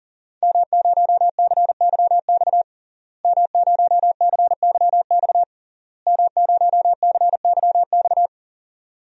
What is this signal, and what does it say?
Morse code, heard as a keyed tone, about 40 wpm, 700 Hz. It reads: M0CYX M0CYX M0CYX
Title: DXCC-100 Call Signs - 5 Characters - Repeated 3x - 40wpm